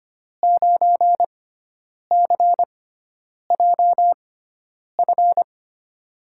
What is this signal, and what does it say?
Morse code, heard as a keyed tone, about 25 wpm, spaced as for 9 wpm, 700 Hz. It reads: 9CJF
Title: USA Call Signs - 5 Characters - 6x Character Spacing - 25wpm